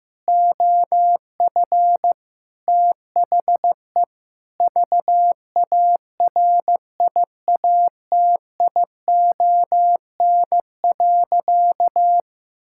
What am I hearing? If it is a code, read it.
Morse code, heard as a keyed tone, about 15 wpm, 700 Hz. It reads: OF THE VARIATION.
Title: Sentences from Top 1600 Words 15wpm